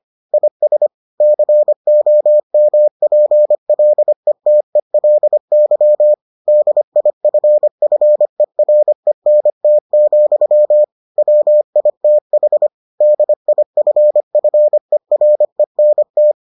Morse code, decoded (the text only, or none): IS COMPLETELY DIFFERENT, WITH DIFFERENT